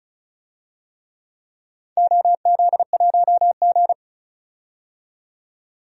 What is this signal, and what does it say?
Morse code, heard as a keyed tone, about 36 wpm, 700 Hz. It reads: OZ1G
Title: DXCC-100 Call Signs - 4 Characters - 35wpm